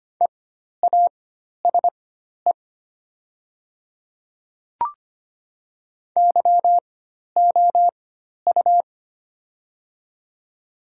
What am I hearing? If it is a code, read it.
Morse code, heard as a keyed tone, about 25 wpm, spaced as for 12 wpm, 700 Hz. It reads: EASE YOU
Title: 4x Character Spacing - Top 300 Words - 25wpm